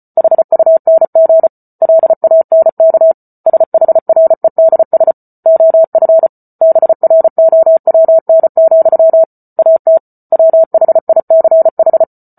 Morse code, heard as a keyed tone, about 34 wpm, 650 Hz. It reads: HUNG LANK SHREDS OF BROWN, AT WHICH
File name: Book Two / 08-Chapter 8 - 34 WPM